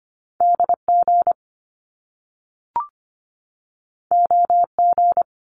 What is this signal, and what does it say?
Morse code, heard as a keyed tone, about 25 wpm, 700 Hz. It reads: DG OG
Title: Top Two Letter Combinations in Words 25wpm